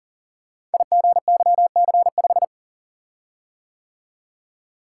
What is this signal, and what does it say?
Morse code, heard as a keyed tone, about 38 wpm, 690 Hz. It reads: IGYC5
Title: Code Groups - 5 Letter:Numbers - 40wpm